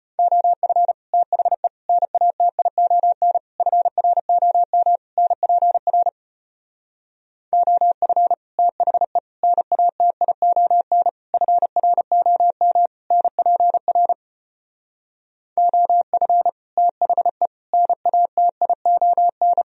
Morse code, heard as a keyed tone, about 33 wpm, 700 Hz. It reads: OF THE NATION FROM NPR OF THE NATION FROM NPR OF THE NATION